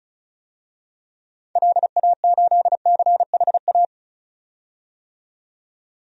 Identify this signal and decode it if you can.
Morse code, heard as a keyed tone, about 36 wpm, 700 Hz. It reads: LA8CHA
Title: DXCC-100 Call Signs - 6 Characters - 35wpm